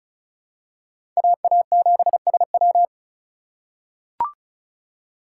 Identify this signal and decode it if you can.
Morse code, heard as a keyed tone, about 35 wpm, 700 Hz. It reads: AA7SW
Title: USA Call Signs - 5 Characters - 3x Character Spacing - 35wpm